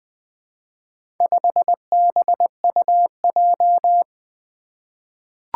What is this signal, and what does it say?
Morse code, heard as a keyed tone, about 20 wpm, 700 Hz. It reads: 5BUJ
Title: Four Challenging Letter-Number Combinations 20wpm